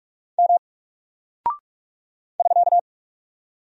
Morse code, heard as a keyed tone, about 46 wpm, 700 Hz. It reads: M <SK>
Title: Single Character (Letter:Number:Punctuation:Prosign) Alt 45wpm